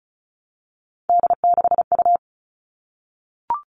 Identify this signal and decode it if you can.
Morse code, heard as a keyed tone, about 34 wpm, 700 Hz. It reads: D6U
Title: All Cut Numbers - 3-Digit 35wpm